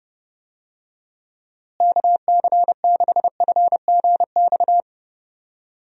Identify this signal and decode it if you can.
Morse code, heard as a keyed tone, about 30 wpm, 700 Hz. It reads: KC6FGX
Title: USA Call Signs - 6 Characters - Repeated 3x - 30wpm